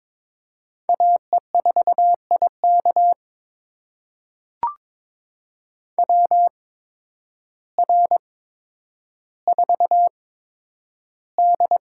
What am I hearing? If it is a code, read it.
Morse code, heard as a keyed tone, about 23 wpm, spaced as for 10 wpm, 700 Hz. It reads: AE4IK WR4D
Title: USA Call Signs - 8x Character Spacing - 22wpm